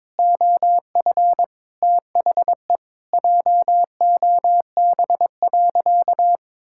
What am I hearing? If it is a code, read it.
Morse code, heard as a keyed tone, about 22 wpm, 700 Hz. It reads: OF THE JOB.
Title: Sentences from Top 1000 Words - Review Part 5 - 22wpm